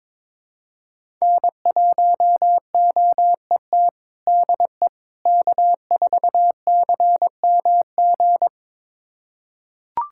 Morse code, heard as a keyed tone, about 22 wpm, 700 Hz. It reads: N1OET DE K4CMG
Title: USA Callsign DE Callsign - 5 Characters - 22wpm